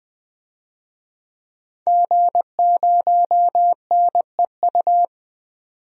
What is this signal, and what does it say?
Morse code, heard as a keyed tone, about 21 wpm, 700 Hz. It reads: G0NEU
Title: DXCC-100 Call Signs - 5 Characters - 20wpm